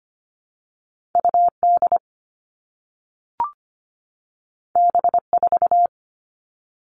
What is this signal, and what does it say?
Morse code, heard as a keyed tone, about 25 wpm, 700 Hz. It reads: UD B4
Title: All Cut Numbers - 2-Digit 25wpm